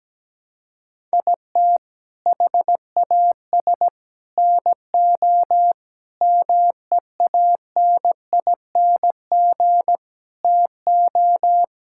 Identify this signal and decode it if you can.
Morse code, heard as a keyed tone, about 17 wpm, 700 Hz. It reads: IT HAS NO MEANING TO
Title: Sentences from Top 1000 Words - Review Part 6 - 17wpm